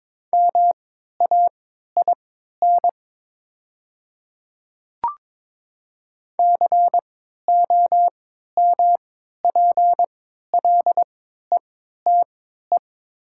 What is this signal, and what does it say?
Morse code, heard as a keyed tone, about 22 wpm, spaced as for 12 wpm, 700 Hz. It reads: MAIN COMPLETE
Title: 3x Character Spacing - Top 500 Words - 22wpm